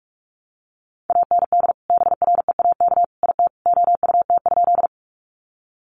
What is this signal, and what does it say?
Morse code, heard as a keyed tone, about 44 wpm, 700 Hz. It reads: AND BREAK IT OUT?
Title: Sentences from Top 900 Words 45wpm